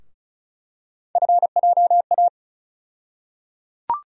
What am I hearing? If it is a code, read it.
Morse code, heard as a keyed tone, about 34 wpm, 700 Hz. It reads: FJA
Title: USA Call Sign Postfixes - 3 Letters 35wpm